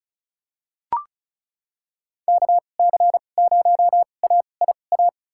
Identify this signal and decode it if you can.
Morse code, heard as a keyed tone, about 36 wpm, spaced as for 25 wpm, 700 Hz. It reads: KC0AIA
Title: USA Call Signs - 2x Character Spacing - 35wpm